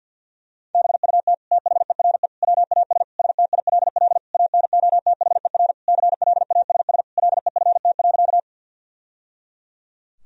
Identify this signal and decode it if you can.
Morse code, heard as a keyed tone, about 50 wpm, 700 Hz. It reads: BUT THERE WAS STILL ANOTHER CLASS LEFT.
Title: Sentences from Top 1000 Words - Review Part 2 - 50wpm